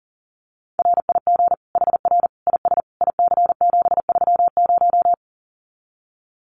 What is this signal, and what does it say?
Morse code, heard as a keyed tone, about 40 wpm, 700 Hz. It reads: RIG HR IS IC730
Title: Common QSO Phrases 40wpm